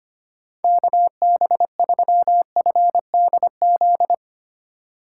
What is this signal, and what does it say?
Morse code, heard as a keyed tone, about 25 wpm, 700 Hz. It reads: KB3FDZ
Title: USA Call Signs - 6 Characters - 2x Character Spacing - 25wpm